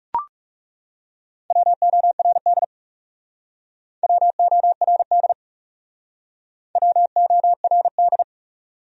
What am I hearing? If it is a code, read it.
Morse code, heard as a keyed tone, about 38 wpm, 700 Hz. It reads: WORD WORD WORD